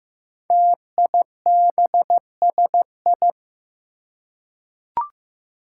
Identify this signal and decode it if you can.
Morse code, heard as a keyed tone, about 15 wpm, 700 Hz. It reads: TIBSI
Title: Five Challenging Letter-Number Combinations 15wpm